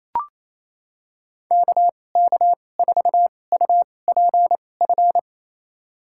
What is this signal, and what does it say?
Morse code, heard as a keyed tone, about 28 wpm, spaced as for 20 wpm, 700 Hz. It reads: KK4UPF
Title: USA Call Signs - 2x Character Spacing - 28wpm